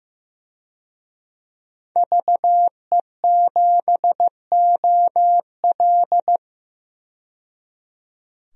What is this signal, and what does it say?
Morse code, heard as a keyed tone, about 16 wpm, 700 Hz. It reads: VE7OL
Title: DXCC-100 Call Signs - 5 Characters - 15wpm